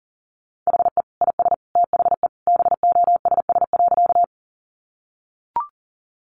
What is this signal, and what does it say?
Morse code, heard as a keyed tone, about 40 wpm, 700 Hz. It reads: HE IS THE BOSS.
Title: Sentences from Top 700 Words 40wpm